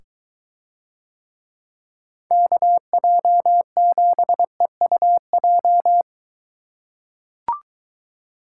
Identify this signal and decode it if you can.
Morse code, heard as a keyed tone, about 23 wpm, 700 Hz. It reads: KJ7EUJ